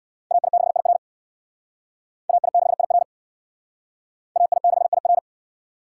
Morse code, heard as a keyed tone, about 60 wpm, 700 Hz. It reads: AI6IR AI6IR AI6IR
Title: Speed Racing - USA 2x2 Call Signs 50wpm